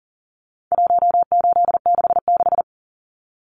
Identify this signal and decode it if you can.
Morse code, heard as a keyed tone, about 40 wpm, 700 Hz. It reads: 1866
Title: Numbers - 4-Digit 40wpm